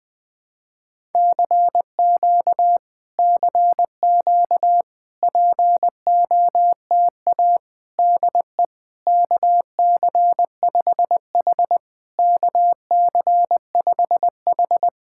Morse code, heard as a keyed tone, about 20 wpm, 700 Hz. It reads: CQ CQ POTA DE KC5H KC5H
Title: POTA QSO Practice - 20wpm